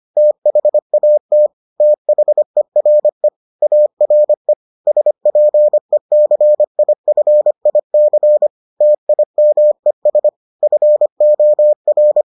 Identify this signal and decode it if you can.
Morse code, heard as a keyed tone, about 25 wpm, 600 Hz. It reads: THAT THERE ARE SPECIFIC TIMES FOR